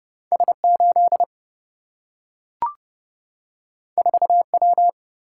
Morse code, heard as a keyed tone, about 30 wpm, 700 Hz. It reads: S 8 4 W
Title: Single-Letter Single-Number 30wpm